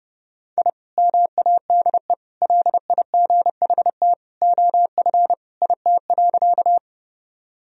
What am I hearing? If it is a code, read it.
Morse code, heard as a keyed tone, about 30 wpm, 700 Hz. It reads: I MADE LIGHT OF IT.
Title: Sentences from Top 1000 Words - Review Part 1 - 30wpm